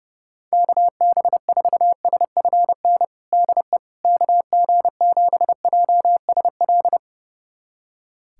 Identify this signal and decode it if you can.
Morse code, heard as a keyed tone, about 29 wpm, 700 Hz. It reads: KB4SFN DE KG7JSL
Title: USA Callsign DE Callsign - 6 Characters - 30wpm